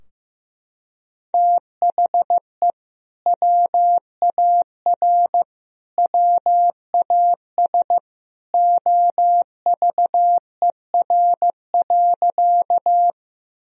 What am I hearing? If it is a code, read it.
Morse code, heard as a keyed tone, about 15 wpm, 700 Hz. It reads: THE WAR WAS OVER.
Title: Sentences from Top 1000 Words - Review Part 2 - 15wpm